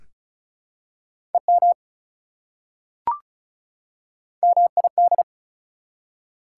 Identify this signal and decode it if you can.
Morse code, heard as a keyed tone, about 34 wpm, 700 Hz. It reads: EM MID
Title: Prefixes and Suffixes 35wpm